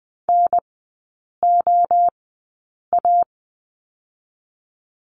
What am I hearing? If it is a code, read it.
Morse code, heard as a keyed tone, about 20 wpm, spaced as for 8 wpm, 700 Hz. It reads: NOA
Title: 035 Lesson 7e - Letter s - Three Characters 20wpm